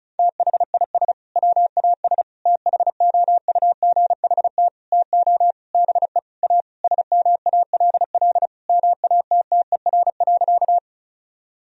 Morse code, heard as a keyed tone, about 36 wpm, 700 Hz. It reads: THIS WAS THOUGHT TO BE A SMALL MATTER.
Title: Sentences from Top 1000 Words - Review Part 1 - 35wpm